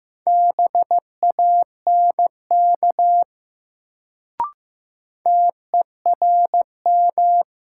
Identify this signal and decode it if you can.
Morse code, heard as a keyed tone, about 15 wpm, 700 Hz. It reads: BANK TERM
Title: Sentences from Top 900 Words - New Word Drill - 15wpm